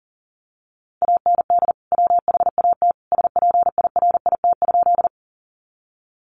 Morse code, heard as a keyed tone, about 37 wpm, 700 Hz. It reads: AND WHAT SPIRIT?
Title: Sentences from Top 1400 Words 40wpm